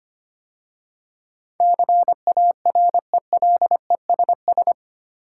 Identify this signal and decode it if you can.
Morse code, heard as a keyed tone, about 25 wpm, 700 Hz. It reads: CARELESS
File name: 8 Letter Words - First 3 Letters Given - 25wpm